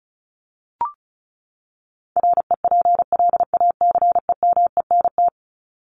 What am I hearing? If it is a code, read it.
Morse code, heard as a keyed tone, about 34 wpm, 700 Hz. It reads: REPLACEMENT
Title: Top 1751-2000 Words 35wpm